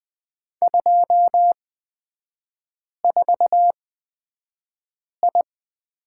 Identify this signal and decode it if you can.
Morse code, heard as a keyed tone, about 20 wpm, spaced as for 5 wpm, 700 Hz. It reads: 24I
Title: 250 Lesson 40e - BK - Three Characters 20wpm